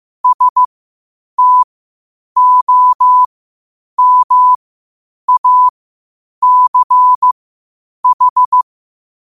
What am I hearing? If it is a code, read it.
Morse code, heard as a keyed tone, about 15 wpm, spaced as for 8 wpm, 1000 Hz. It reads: STOMACH
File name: qotdfiles / 8wpm